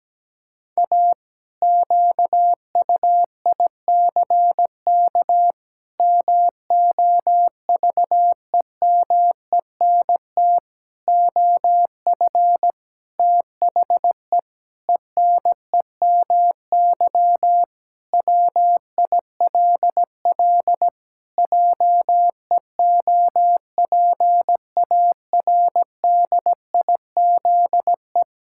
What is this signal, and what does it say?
Morse code, heard as a keyed tone, about 17 wpm, 700 Hz. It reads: A QUICK MOVEMENT OF THE ENEMY WILL JEOPARDIZE
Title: Pangram Challenge v2 - 17wpm